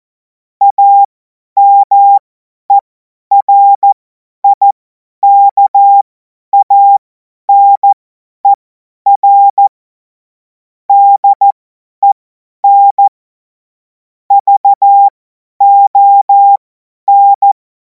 Morse code, heard as a keyed tone, about 14 wpm, spaced as for 10 wpm, 800 Hz. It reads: AMERIKANER DEN VON
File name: Morsekurs / Witze / Witz 28. April 2026 50 F70